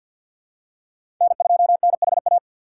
Morse code, heard as a keyed tone, about 53 wpm, 700 Hz. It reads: N2NHA
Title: USA Call Signs - 3x Character Spacing - 50wpm